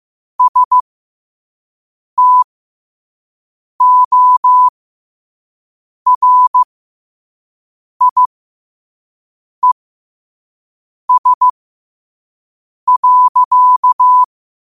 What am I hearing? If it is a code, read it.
Morse code, heard as a keyed tone, about 15 wpm, spaced as for 5 wpm, 1000 Hz. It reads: STORIES.
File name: qotdfiles / 5wpm